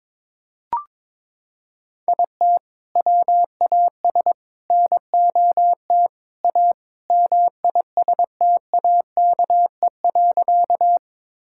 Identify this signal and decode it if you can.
Morse code, heard as a keyed tone, about 22 wpm, 700 Hz. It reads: IT WAS NOT A MISTAKE.
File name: practice / 22wpm / Sentences with 4-Word Phrases - 22wpm